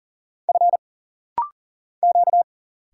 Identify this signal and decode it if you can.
Morse code, heard as a keyed tone, about 40 wpm, 700 Hz. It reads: F Q
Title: Single Letters - Rapid-Fire 40wpm